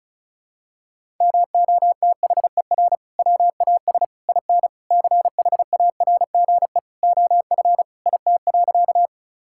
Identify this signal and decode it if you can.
Morse code, heard as a keyed tone, about 34 wpm, 700 Hz. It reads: MOTHER WAS IN CHARGE OF IT.
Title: Sentences from Top 1000 Words - Review Part 6 - 35wpm